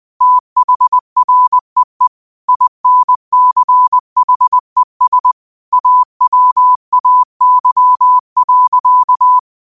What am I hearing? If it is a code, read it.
Morse code, heard as a keyed tone, about 20 wpm, 1000 Hz. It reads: THREE INCHES AWAY.
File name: qotdfiles / 20wpm